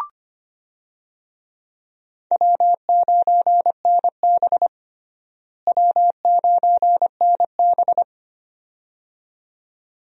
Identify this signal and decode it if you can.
Morse code, heard as a keyed tone, about 25 wpm, 700 Hz. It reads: W9NB W9NB
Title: USA Call Signs - 4 Characters - Repeated 2x - 25wpm